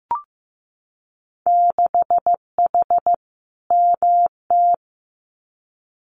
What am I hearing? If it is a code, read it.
Morse code, heard as a keyed tone, about 15 wpm, 700 Hz. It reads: 6H MT
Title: Winter Field Day Exchanges Encore 15wpm